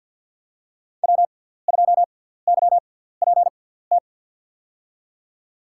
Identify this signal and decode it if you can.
Morse code, heard as a keyed tone, about 48 wpm, spaced as for 17 wpm, 700 Hz. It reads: W2YPT
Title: USA Call Signs - 5 Characters - 6x Character Spacing - 50wpm